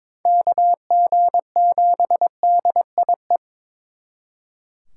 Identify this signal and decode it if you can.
Morse code, heard as a keyed tone, about 26 wpm, 700 Hz. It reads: KG7DIE
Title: Speed Racing - USA 2x3 Call Signs 22wpm